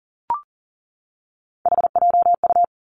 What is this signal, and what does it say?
Morse code, heard as a keyed tone, about 38 wpm, 700 Hz. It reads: HJU
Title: Three Challenging Letter-Number Combinations 40wpm